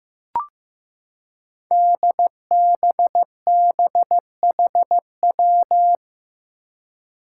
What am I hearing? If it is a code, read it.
Morse code, heard as a keyed tone, about 15 wpm, 700 Hz. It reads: DBBHW
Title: Five Challenging Letter-Number Combinations 15wpm